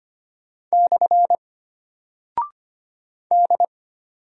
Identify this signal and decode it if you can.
Morse code, heard as a keyed tone, about 25 wpm, 700 Hz. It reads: / D
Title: Single Character (Letter:Number:Punctuation:Prosign) Alt 25wpm